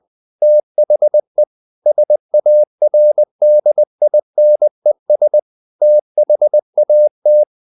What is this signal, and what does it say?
Morse code, heard as a keyed tone, about 20 wpm, 600 Hz. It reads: THE SARDINES THAT
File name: the old man and the sea / the old man and the sea part 1 20wpm